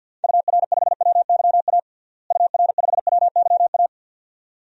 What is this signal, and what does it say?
Morse code, heard as a keyed tone, about 52 wpm, 700 Hz. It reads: UR5WYA UR5WYA
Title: DXCC-100 Call Signs - 6 Characters - Repeated 2x - 50wpm